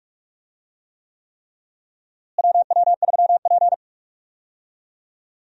Morse code, heard as a keyed tone, about 44 wpm, 700 Hz. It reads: WW3P
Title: USA Call Signs - 4 Characters - Repeated 3x - 45wpm